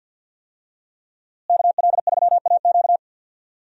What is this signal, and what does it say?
Morse code, heard as a keyed tone, about 49 wpm, 700 Hz. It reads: KL3AQ
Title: USA Call Signs - 5 Characters - Repeated 3x - 50wpm